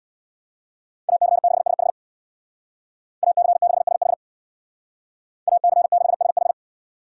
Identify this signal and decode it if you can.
Morse code, heard as a keyed tone, about 61 wpm, 700 Hz. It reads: AC6SH AC6SH AC6SH
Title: Speed Racing - USA 2x2 Call Signs 50wpm